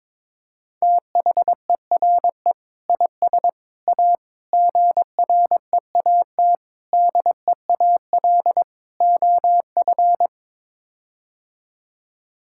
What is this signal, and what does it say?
Morse code, heard as a keyed tone, about 22 wpm, 700 Hz. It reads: THERE IS A GREAT DEAL OF
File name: practice / 22wpm / Sets of 6 Words - 22wpm